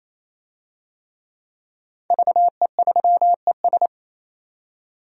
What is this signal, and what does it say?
Morse code, heard as a keyed tone, about 27 wpm, 700 Hz. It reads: VE3ES
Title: DXCC-100 Call Signs - 5 Characters - 28wpm